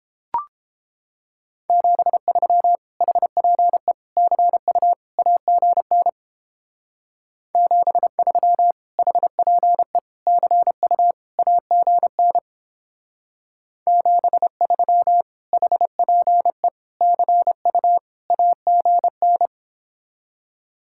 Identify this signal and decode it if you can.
Morse code, heard as a keyed tone, about 29 wpm, 700 Hz. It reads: 73 HPE CU AGN 73 HPE CU AGN 73 HPE CU AGN